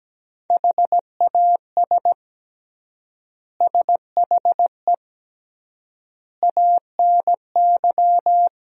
Morse code, heard as a keyed tone, about 17 wpm, 700 Hz. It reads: HAS SHE ANY
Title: 3x Word-Spacing - Sentences from Top 400 Words - 17wpm